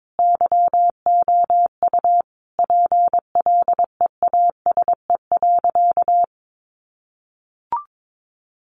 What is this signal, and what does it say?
Morse code, heard as a keyed tone, about 22 wpm, 700 Hz. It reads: YOU PLEASE.
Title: Sentences from Top 1500 Words 22wpm